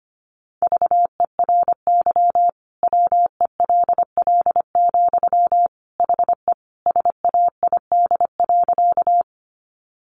Morse code, heard as a keyed tone, about 25 wpm, 700 Hz. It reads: VERY WELL, HE SAID.
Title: Sentences from Top 200 Words 25wpm